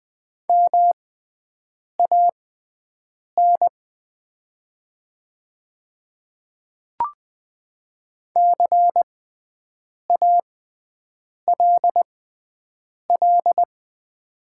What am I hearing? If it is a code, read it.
Morse code, heard as a keyed tone, about 20 wpm, spaced as for 7 wpm, 700 Hz. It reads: MAN CALL
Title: 6x Character Spacing - Top 300 Words - 20wpm